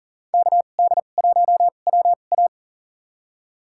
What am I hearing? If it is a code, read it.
Morse code, heard as a keyed tone, about 40 wpm, spaced as for 29 wpm, 700 Hz. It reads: KD1WA
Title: USA Call Signs - 5 Characters - 2x Character Spacing - 40wpm